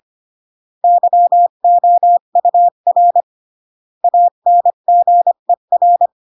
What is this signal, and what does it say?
Morse code, heard as a keyed tone, about 25 wpm, 700 Hz. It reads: YOUR ANGER